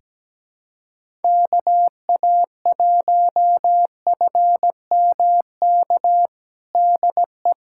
Morse code, heard as a keyed tone, about 17 wpm, 700 Hz. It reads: KA1FMK DE
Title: USA Callsign DE Callsign - 6 Characters - 17wpm